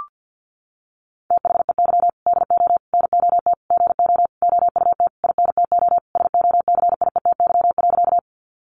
Morse code, heard as a keyed tone, about 53 wpm, 700 Hz. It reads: THEY DO NOT GO OUT INTO SOCIETY.
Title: Sentences from Top 700 Words 50wpm